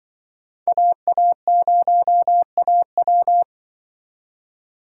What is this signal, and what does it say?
Morse code, heard as a keyed tone, about 23 wpm, 700 Hz. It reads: AA0AW
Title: Speed Racing - USA 2x2 Call Signs 20wpm